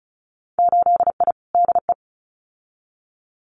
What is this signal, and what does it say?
Morse code, heard as a keyed tone, about 36 wpm, 700 Hz. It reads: 8I DE
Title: Winter Field Day Exchanges Encore 35wpm